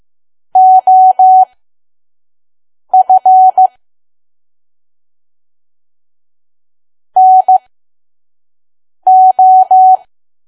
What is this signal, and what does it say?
Morse code, heard as a keyed tone, about 15 wpm, spaced as for 5 wpm, 750 Hz. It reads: OF NO
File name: Archive / 5 WPM files / PM